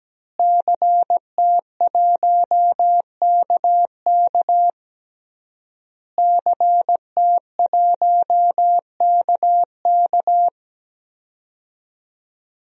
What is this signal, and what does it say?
Morse code, heard as a keyed tone, about 17 wpm, 700 Hz. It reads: CT1KK CT1KK
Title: DXCC-100 Call Signs - 5 Characters - Repeated 3x - 17wpm